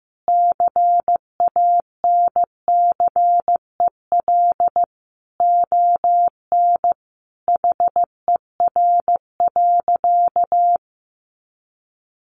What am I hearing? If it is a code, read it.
Morse code, heard as a keyed tone, about 15 wpm, 700 Hz. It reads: CANCEL ON HER.
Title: Sentences from Top 1200 Words 15wpm